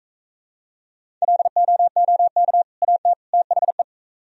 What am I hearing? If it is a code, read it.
Morse code, heard as a keyed tone, about 44 wpm, 700 Hz. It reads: LOOK AT THE